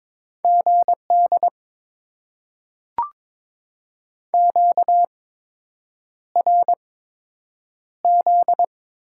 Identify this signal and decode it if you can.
Morse code, heard as a keyed tone, about 22 wpm, spaced as for 8 wpm, 700 Hz. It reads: GD QRZ
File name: Top 90 QSO Elements - 8x Character Spacing - 22wpm